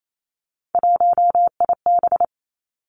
Morse code, heard as a keyed tone, about 28 wpm, 700 Hz. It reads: 1IB
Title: Three Challenging Letter-Number Combinations 28wpm